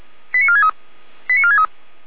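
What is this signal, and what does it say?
Five-tone selective call (ZVEI), heard as quick runs of five tones, tones of about 70 ms. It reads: 89463 89463
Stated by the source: ZVEI five-tone selective call 89456, received by SDR.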